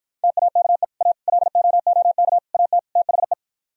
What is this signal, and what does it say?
Morse code, heard as a keyed tone, about 52 wpm, 700 Hz. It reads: TAKE A LOOK AT THE